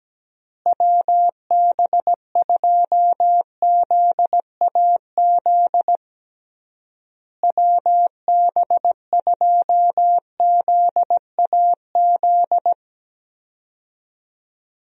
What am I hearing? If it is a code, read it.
Morse code, heard as a keyed tone, about 17 wpm, 700 Hz. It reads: WB2ZAZ WB2ZAZ
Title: USA Call Signs - 6 Characters - Repeated 3x - 17wpm